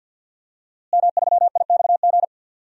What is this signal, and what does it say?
Morse code, heard as a keyed tone, about 52 wpm, 700 Hz. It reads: M3IXG
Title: DXCC-100 Call Signs - 5 Characters - Repeated 3x - 50wpm